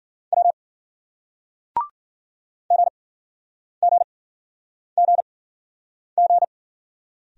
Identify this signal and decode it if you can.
Morse code, heard as a keyed tone, about 50 wpm, spaced as for 9 wpm, 700 Hz. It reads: W GGGG
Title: Speed-Racing Single Letters 40wpm